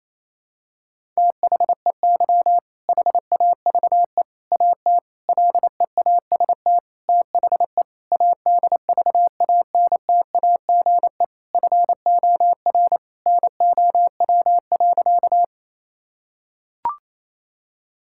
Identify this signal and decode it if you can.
Morse code, heard as a keyed tone, about 27 wpm, 700 Hz. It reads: THEY HAVE AT LEAST THE ADVANTAGE FOR NOW.
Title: Sentences from Top 1000 Words - Review Part 6 - 28wpm